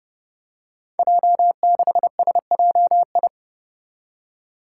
Morse code, heard as a keyed tone, about 30 wpm, 700 Hz. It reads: J6SJI
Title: Five Challenging Letter-Number Combinations 30wpm